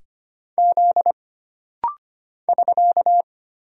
Morse code, heard as a keyed tone, about 25 wpm, 700 Hz. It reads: Z <SK>
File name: Single Character (Letter:Number:Punctuation:Prosign) - Rapid-Fire - 25wpm